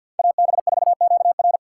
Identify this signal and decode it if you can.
Morse code, heard as a keyed tone, about 52 wpm, 700 Hz. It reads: AB4QR
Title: USA Callsign DE Callsign - 5 Characters - 50wpm